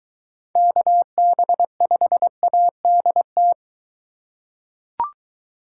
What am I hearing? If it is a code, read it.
Morse code, heard as a keyed tone, about 23 wpm, 700 Hz. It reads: KB5ADT